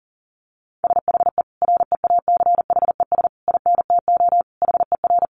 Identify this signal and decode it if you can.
Morse code, heard as a keyed tone, about 40 wpm, 700 Hz. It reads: SHE REACHES INTO HER
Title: Additional Easy Phrases 40wpm